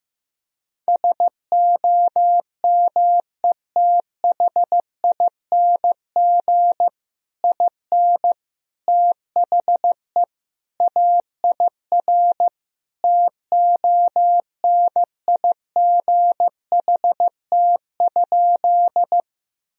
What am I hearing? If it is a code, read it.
Morse code, heard as a keyed tone, about 15 wpm, 700 Hz. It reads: SOMETHING IN THE AIR TONIGHT?
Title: Sentences from Top 1000 Words - Review Part 1 - 15wpm